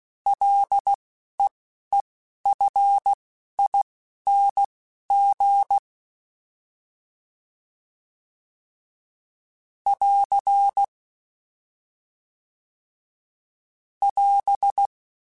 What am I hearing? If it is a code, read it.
Morse code, heard as a keyed tone, about 16 wpm, spaced as for 11 wpm, 800 Hz. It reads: LEEFING <AR> <AS>
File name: F and L / F and L mp3 / F and L Prefixes 3x2 Twice @10